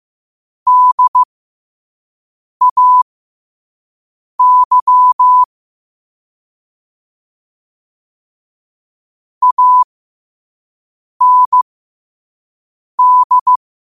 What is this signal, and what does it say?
Morse code, heard as a keyed tone, about 15 wpm, spaced as for 5 wpm, 1000 Hz. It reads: DAY AND
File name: qotdfiles / 5wpm